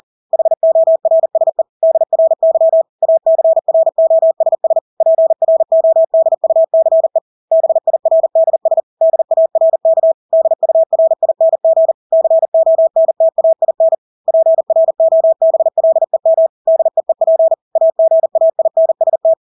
Morse code, heard as a keyed tone, about 40 wpm, 650 Hz. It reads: HORSE DRY ACROSS PRODUCE BIRDS DARK DURING CONTAIN PROBLEM DEEP AGAINST
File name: Engelske ord 40-40 wpm